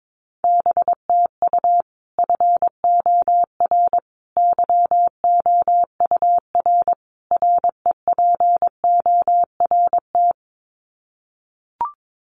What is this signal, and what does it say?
Morse code, heard as a keyed tone, about 22 wpm, 700 Hz. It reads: BTU FOR YOUR REPORT
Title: Common QSO Phrases 22wpm